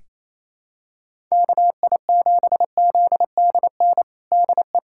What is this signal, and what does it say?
Morse code, heard as a keyed tone, about 28 wpm, 700 Hz. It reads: KI7ZDN DE
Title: USA Callsign DE Callsign - 6 Characters - 28wpm